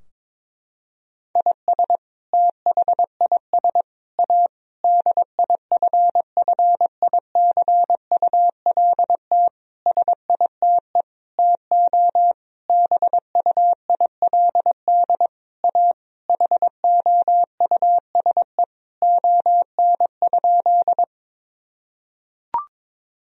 Morse code, heard as a keyed tone, about 21 wpm, 700 Hz. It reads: IS THIS A DIFFICULT SITE TO BUILD A HOUSE ON?
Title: Sentences from Top 1000 Words - Review Part 4 - 22wpm